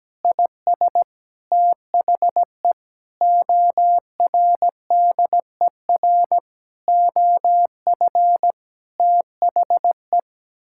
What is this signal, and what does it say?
Morse code, heard as a keyed tone, about 17 wpm, 700 Hz. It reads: IS THE ORDER OF THE
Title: Sentences from Top 1000 Words - Review Part 5 - 17wpm